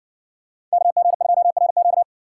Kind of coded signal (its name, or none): Morse code